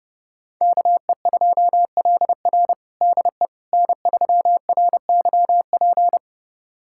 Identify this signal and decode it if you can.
Morse code, heard as a keyed tone, about 30 wpm, 700 Hz. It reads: KE2LR DE N3RYP
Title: USA Callsign DE Callsign - 5 Characters - 30wpm